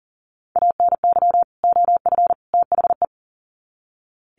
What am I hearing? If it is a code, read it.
Morse code, heard as a keyed tone, about 40 wpm, 700 Hz. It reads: ANY OF THE
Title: Sets of 3 Words 40wpm